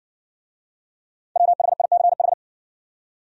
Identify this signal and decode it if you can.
Morse code, heard as a keyed tone, about 55 wpm, 700 Hz. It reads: WHICH